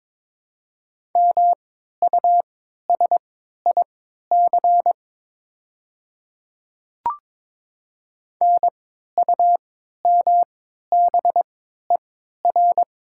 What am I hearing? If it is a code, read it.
Morse code, heard as a keyed tone, about 23 wpm, spaced as for 13 wpm, 700 Hz. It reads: MUSIC NUMBER
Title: 3x Character Spacing - Top 300 Words - 22wpm